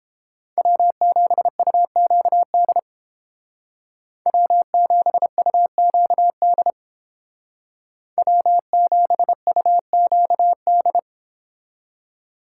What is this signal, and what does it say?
Morse code, heard as a keyed tone, about 30 wpm, 700 Hz. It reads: W7UQD W7UQD W7UQD